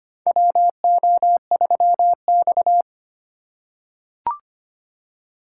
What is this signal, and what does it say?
Morse code, heard as a keyed tone, about 25 wpm, 700 Hz. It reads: WO3X
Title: USA Call Signs - 4 Characters - 2x Character Spacing - 25wpm